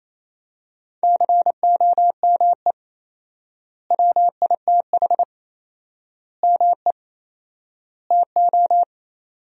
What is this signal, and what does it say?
Morse code, heard as a keyed tone, about 27 wpm, 700 Hz. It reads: COME WITH ME TO
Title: 4x Word-Spacing - Sentences from Top 200 Words - 28wpm